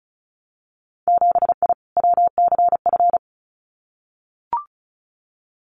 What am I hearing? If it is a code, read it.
Morse code, heard as a keyed tone, about 34 wpm, 700 Hz. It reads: 7I WCF
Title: Winter Field Day Exchanges Encore 35wpm